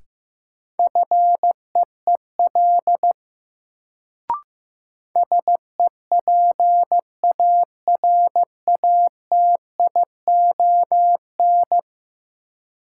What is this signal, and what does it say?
Morse code, heard as a keyed tone, about 15 wpm, 700 Hz. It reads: FEEL SEPARATION
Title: Sets of 5 Words - New Word Drill - 15wpm